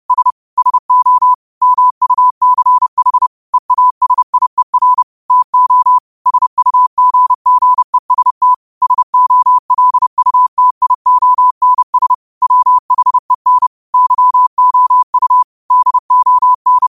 Morse code, heard as a keyed tone, about 30 wpm, 1000 Hz. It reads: S SO MUCH EASIER TO SUGGEST SOLUTIONS WHEN YOU DON
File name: qotdfiles / 30wpm